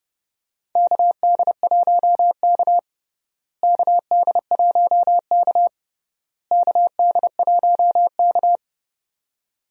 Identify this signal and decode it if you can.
Morse code, heard as a keyed tone, about 32 wpm, 700 Hz. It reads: KD1K KD1K KD1K
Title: USA Call Signs - 4 Characters - Repeated 3x - 30wpm